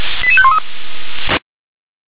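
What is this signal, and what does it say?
Five-tone selective call (ZVEI), heard as quick runs of five tones, tones of about 70 ms. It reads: 88513
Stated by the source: ZVEI five-tone selective call 88731 received by SDR.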